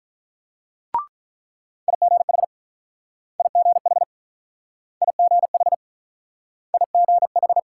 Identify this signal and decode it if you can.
Morse code, heard as a keyed tone, about 44 wpm, 700 Hz. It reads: IGH IGH IGH IGH